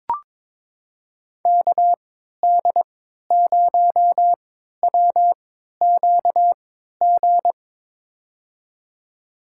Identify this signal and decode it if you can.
Morse code, heard as a keyed tone, about 22 wpm, spaced as for 13 wpm, 700 Hz. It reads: KD0WQG
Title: USA Call Signs - 3x Character Spacing - 22wpm